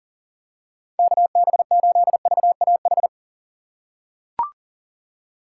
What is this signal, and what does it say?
Morse code, heard as a keyed tone, about 40 wpm, 700 Hz. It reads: KB8VAH
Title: USA Call Signs - 6 Characters - 6x Character Spacing - 40wpm